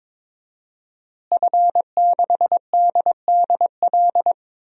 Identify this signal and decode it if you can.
Morse code, heard as a keyed tone, about 23 wpm, 700 Hz. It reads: F6DDL
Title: DXCC-100 Call Signs - 5 Characters - Repeated 3x - 22wpm